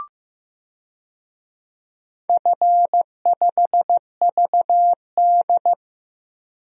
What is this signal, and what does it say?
Morse code, heard as a keyed tone, about 15 wpm, 700 Hz. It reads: F5VD
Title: DXCC-100 Call Signs - 4 Characters - Repeated 2x - 15wpm